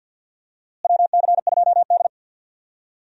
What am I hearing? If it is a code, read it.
Morse code, heard as a keyed tone, about 51 wpm, 700 Hz. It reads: WC2D
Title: USA Call Signs - 4 Characters - Repeated 3x - 50wpm